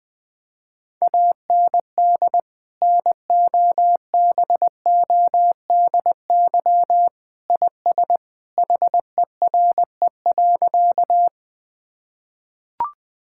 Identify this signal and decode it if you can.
Morse code, heard as a keyed tone, about 20 wpm, 700 Hz. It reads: AND NOBODY IS HERE.
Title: Sentences from Top 1600 Words 20wpm